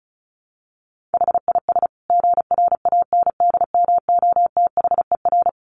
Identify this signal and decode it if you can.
Morse code, heard as a keyed tone, about 34 wpm, 700 Hz. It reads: HIS GRANDMOTHER
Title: Additional Easy Phrases 35wpm